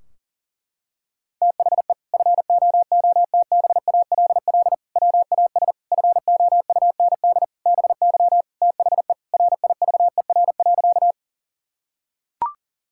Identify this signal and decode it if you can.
Morse code, heard as a keyed tone, about 37 wpm, 700 Hz. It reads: THE FOOTBALL WAS FOUND BY THE RIVER.
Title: Sentences from Top 1600 Words 40wpm